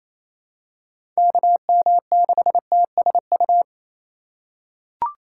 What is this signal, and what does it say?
Morse code, heard as a keyed tone, about 28 wpm, 700 Hz. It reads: KM6TSU
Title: USA Call Signs - 6 Characters - 3x Character Spacing - 28wpm